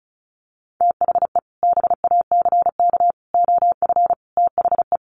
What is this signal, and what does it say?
Morse code, heard as a keyed tone, about 34 wpm, 700 Hz. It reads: THE BACK OF THE
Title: Sets of 5 Words Encore 35wpm